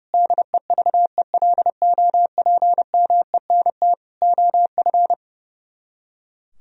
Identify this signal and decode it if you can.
Morse code, heard as a keyed tone, about 37 wpm, 700 Hz. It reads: DEVELOPMENT OF
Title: Speed-Racing - Sets of 3 Words 30wpm